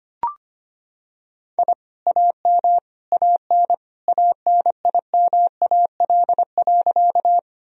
Morse code, heard as a keyed tone, about 25 wpm, 700 Hz. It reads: I AM AN ANIMAL.
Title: Sentences from Top 1000 Words - Review Part 1 - 25wpm